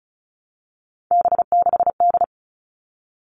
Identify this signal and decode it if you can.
Morse code, heard as a keyed tone, about 34 wpm, 700 Hz. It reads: B6D
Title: All Cut Numbers - 3-Digit 35wpm